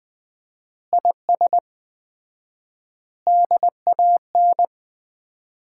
Morse code, heard as a keyed tone, about 20 wpm, 700 Hz. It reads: IS DAN
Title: Common QSO Phrases - 4x Word-Spacing - 20wpm